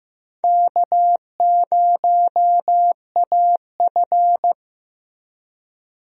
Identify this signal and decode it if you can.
Morse code, heard as a keyed tone, about 15 wpm, 700 Hz. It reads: K0AF
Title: USA Call Signs - 4 Characters - Repeated 2x - 15wpm